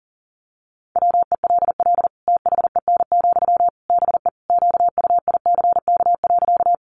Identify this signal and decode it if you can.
Morse code, heard as a keyed tone, about 40 wpm, 700 Hz. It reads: WELL THEN, BE QUICK.
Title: Sentences from Top 500 Words 40wpm